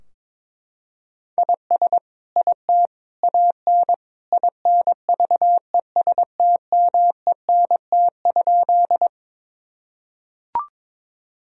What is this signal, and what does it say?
Morse code, heard as a keyed tone, about 22 wpm, 700 Hz. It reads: IS IT AN INVESTMENT?
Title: Sentences from Top 1000 Words - Review Part 4 - 22wpm